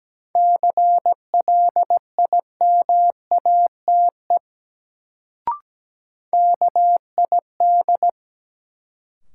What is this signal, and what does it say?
Morse code, heard as a keyed tone, about 17 wpm, 700 Hz. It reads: CLIMATE KID
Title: Sentences from Top 1600 Words - New Word Drill - 17wpm